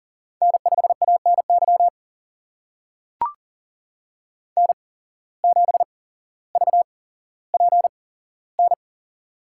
Morse code, heard as a keyed tone, about 40 wpm, spaced as for 17 wpm, 700 Hz. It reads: N5ANY N7VPN
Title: USA Call Signs - 5 Characters - 8x Character Spacing - 40wpm